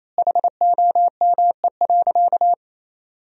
Morse code, heard as a keyed tone, about 28 wpm, 700 Hz. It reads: HOME.
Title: 4x Word-Spacing - Sentences from Top 500 Words - 28wpm